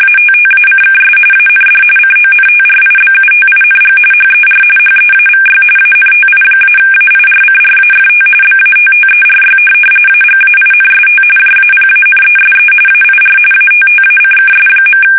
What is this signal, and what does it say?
RTTY, about 45.45 baud. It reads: ATTACKS AND CONTRIBUTING HER HEAVY ANTI-AIRCRAFT BATTERY AGAINST JAPANESE AIR ATTACKS.
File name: fsk850 / USS-Massachusetts